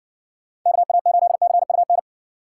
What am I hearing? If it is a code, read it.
Morse code, heard as a keyed tone, about 58 wpm, 700 Hz. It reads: KA8CUN